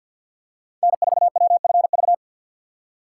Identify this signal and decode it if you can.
Morse code, heard as a keyed tone, about 52 wpm, 700 Hz. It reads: N4WFV
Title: USA Call Signs - 3x Character Spacing - 50wpm